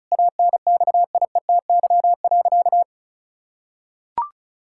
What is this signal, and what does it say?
Morse code, heard as a keyed tone, about 34 wpm, 700 Hz. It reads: ANXIETY.
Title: Sentences from Top 1600 Words 35wpm